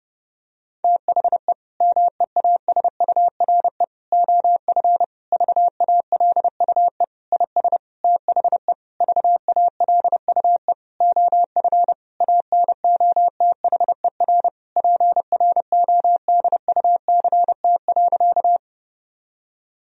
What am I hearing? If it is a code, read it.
Morse code, heard as a keyed tone, about 30 wpm, 700 Hz. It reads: THE MEASURE OF VALUE IS THE VALUE OF ANOTHER PRODUCT.
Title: Sentences from Top 1000 Words - Review Part 3 - 30wpm